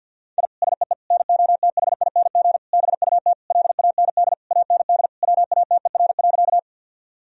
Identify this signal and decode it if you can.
Morse code, heard as a keyed tone, about 51 wpm, 700 Hz. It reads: I SEE NOTHING BUT LAND AND WATER.
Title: Sentences from Top 1000 Words - Review Part 1 - 50wpm